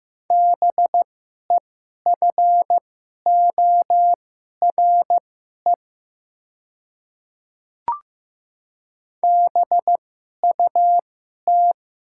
Morse code, heard as a keyed tone, about 15 wpm, spaced as for 11 wpm, 690 Hz. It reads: BEFORE BUT
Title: 2x Character Spacing - Top 200 Words - 15wpm